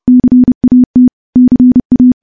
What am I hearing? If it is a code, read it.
Morse code, heard as a keyed tone, about 30 wpm, 250 Hz. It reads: CAT CA
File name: morse / morse-training-30WPM